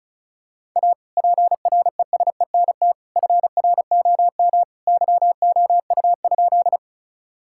Morse code, heard as a keyed tone, about 36 wpm, 690 Hz. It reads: A PRESENT FROM YOU?
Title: Sentences from Top 1000 Words - Review Part 5 - 35wpm